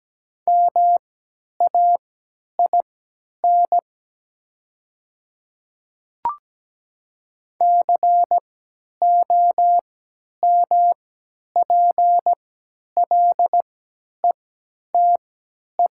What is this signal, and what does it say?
Morse code, heard as a keyed tone, about 17 wpm, spaced as for 10 wpm, 700 Hz. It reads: MAIN COMPLETE
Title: 3x Character Spacing - Top 500 Words - 17wpm